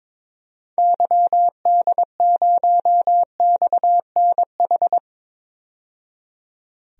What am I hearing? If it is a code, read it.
Morse code, heard as a keyed tone, about 21 wpm, 700 Hz. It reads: YD0XNH
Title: DXCC-100 Call Signs - 6 Characters - 22wpm